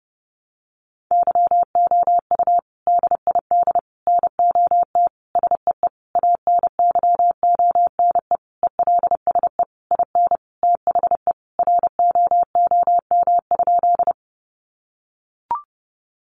Morse code, heard as a keyed tone, about 30 wpm, 700 Hz. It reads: YOU DID NOT SEE ANYONE ELSE IN THE ROOM?
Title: Sentences from Top 900 Words 30wpm